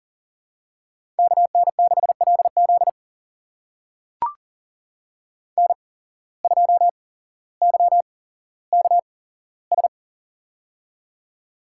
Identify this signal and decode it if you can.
Morse code, heard as a keyed tone, about 40 wpm, spaced as for 17 wpm, 700 Hz. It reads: KN6LZ N2YKS
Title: USA Call Signs - 5 Characters - 8x Character Spacing - 40wpm